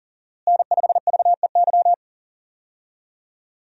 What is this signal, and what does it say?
Morse code, heard as a keyed tone, about 40 wpm, 700 Hz. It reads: N5VEY
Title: USA Call Signs - 5 Characters - Repeated 3x - 40wpm